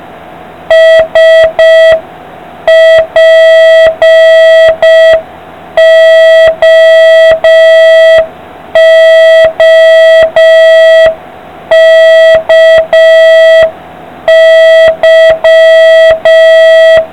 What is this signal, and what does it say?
Morse code, heard as a keyed tone, about 5 wpm, 1950 Hz. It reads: SPOOKY